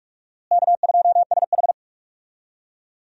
Morse code, heard as a keyed tone, about 41 wpm, 700 Hz. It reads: K2SH
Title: USA Call Signs - 4 Characters - Repeated 3x - 45wpm